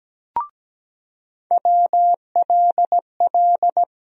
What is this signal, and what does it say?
Morse code, heard as a keyed tone, about 17 wpm, 700 Hz. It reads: WLL
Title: USA Call Sign Postfixes - 3 Letters 17wpm